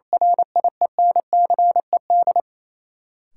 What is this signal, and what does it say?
Morse code, heard as a keyed tone, about 27 wpm, 700 Hz. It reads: RIENCED
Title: Sentences from Top 1100 Words - New Word Drill - 28wpm